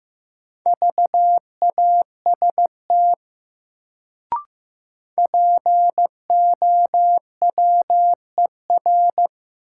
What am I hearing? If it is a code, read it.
Morse code, heard as a keyed tone, about 15 wpm, 700 Hz. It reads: VAST POWER
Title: Sets of 4 Words - New Word Drill - 15wpm